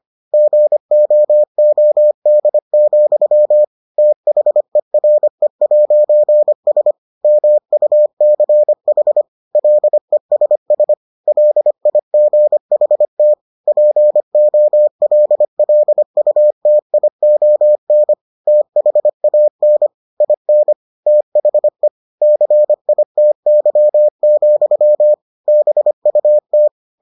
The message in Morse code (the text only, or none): GOOD, THERE'S MUCH LESS LIGHT POLLUTION THAN IN THE CITY, BUT